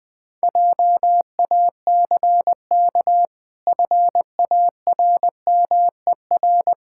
Morse code, heard as a keyed tone, about 20 wpm, 700 Hz. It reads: JACK FARMER
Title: Pangram Challenge v2 - 20wpm